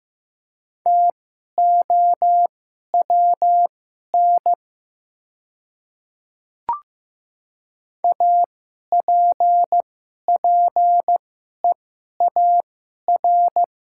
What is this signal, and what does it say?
Morse code, heard as a keyed tone, about 15 wpm, spaced as for 11 wpm, 700 Hz. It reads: TOWN APPEAR
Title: 2x Character Spacing - Top 500 Words - 15wpm